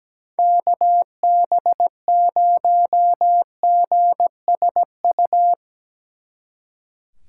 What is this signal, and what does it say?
Morse code, heard as a keyed tone, about 17 wpm, 700 Hz. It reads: KB0GSU
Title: Speed Racing - USA 2x3 Call Signs 17wpm